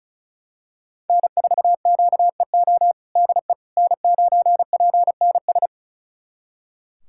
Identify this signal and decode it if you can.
Morse code, heard as a keyed tone, about 36 wpm, 700 Hz. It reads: N4QEO DE N9PNS
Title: USA Callsign DE Callsign - 5 Characters - 35wpm